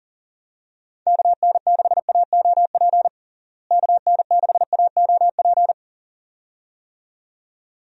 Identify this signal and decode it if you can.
Morse code, heard as a keyed tone, about 40 wpm, 700 Hz. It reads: KN6AOP KN6AOP
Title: USA Call Signs - 6 Characters - Repeated 2x - 40wpm